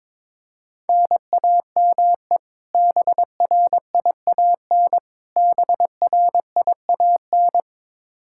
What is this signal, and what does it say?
Morse code, heard as a keyed tone, about 22 wpm, 700 Hz. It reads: NAME BRIAN BRIAN
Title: K1USN SST QSO Practice 22wpm